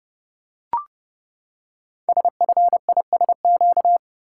Morse code, heard as a keyed tone, about 30 wpm, 700 Hz. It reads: SFISQ
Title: Code Groups - 5 Letter:Numbers - 30wpm